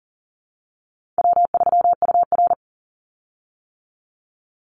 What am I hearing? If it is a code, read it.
Morse code, heard as a keyed tone, about 41 wpm, 700 Hz. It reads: W3UR
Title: Easy Call Signs - 40wpm